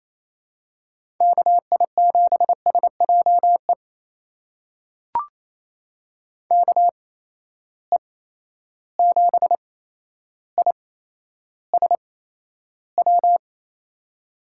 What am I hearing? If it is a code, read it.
Morse code, heard as a keyed tone, about 28 wpm, spaced as for 12 wpm, 700 Hz. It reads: KI7SJE KE7ISW
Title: USA Call Signs - 6 Characters - 8x Character Spacing - 28wpm